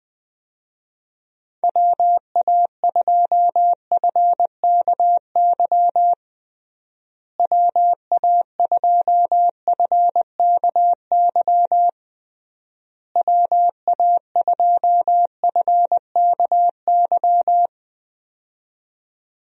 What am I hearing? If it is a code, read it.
Morse code, heard as a keyed tone, about 20 wpm, 700 Hz. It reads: WA2FKY WA2FKY WA2FKY
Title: USA Call Signs - 6 Characters - Repeated 3x - 20wpm